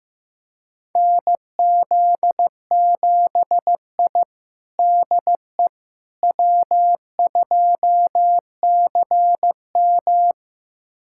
Morse code, heard as a keyed tone, about 15 wpm, 700 Hz. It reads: NZ7I DE W2CM
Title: USA Callsign DE Callsign - 4 Characters - 15wpm